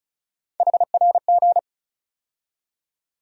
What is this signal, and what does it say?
Morse code, heard as a keyed tone, about 36 wpm, 700 Hz. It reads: HRG
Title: USA Call Sign Postfixes - 3 Letters 35wpm